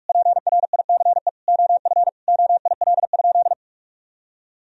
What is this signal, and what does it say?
Morse code, heard as a keyed tone, about 41 wpm, 700 Hz. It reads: PRICE OF OIL?
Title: Sentences from Top 1000 Words - Review Part 4 - 45wpm